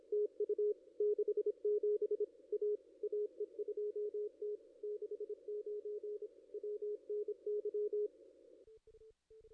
Morse code, heard as a keyed tone, about 27 wpm, 420 Hz. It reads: TU 67 A AE2T 69 WNY
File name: contests / ss-cw-k5tr-op-n5rz -2200